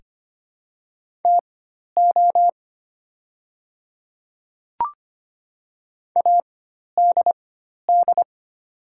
Morse code, heard as a keyed tone, about 25 wpm, spaced as for 12 wpm, 700 Hz. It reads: TO ADD
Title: 4x Character Spacing - Top 200 Words - 25wpm